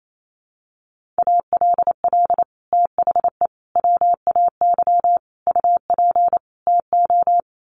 Morse code, heard as a keyed tone, about 28 wpm, 700 Hz. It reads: ALL THE WAY UP TO
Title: Sets of 5 Words Encore 28wpm